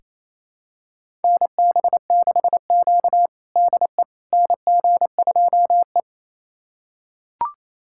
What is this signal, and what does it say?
Morse code, heard as a keyed tone, about 28 wpm, 700 Hz. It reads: NB6Q DE NG2E
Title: USA Callsign DE Callsign - 4 Characters - 28wpm